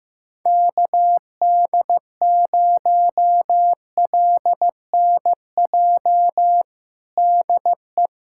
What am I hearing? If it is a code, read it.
Morse code, heard as a keyed tone, about 16 wpm, 700 Hz. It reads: KD0LNJ DE
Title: USA Callsign DE Callsign 15wpm